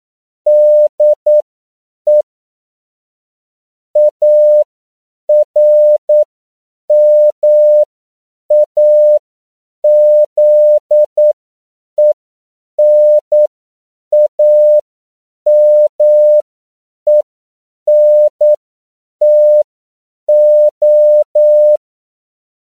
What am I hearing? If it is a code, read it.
Morse code, heard as a keyed tone, about 9 wpm, spaced as for 7 wpm, 600 Hz. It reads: DE ARMAZENAMENTO
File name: CW / 10PPM